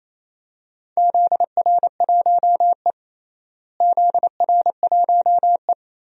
Morse code, heard as a keyed tone, about 27 wpm, 700 Hz. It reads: ZR1E ZR1E
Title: DXCC-100 Call Signs - 4 Characters - Repeated 3x - 28wpm